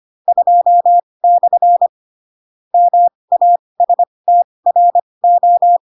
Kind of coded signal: Morse code